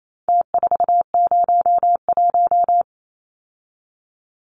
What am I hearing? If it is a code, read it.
Morse code, heard as a keyed tone, about 28 wpm, 700 Hz. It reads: T401
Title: Numbers - Contest Style - 4-Digit 28wpm